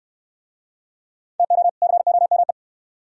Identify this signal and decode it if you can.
Morse code, heard as a keyed tone, about 64 wpm, 700 Hz. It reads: TO COME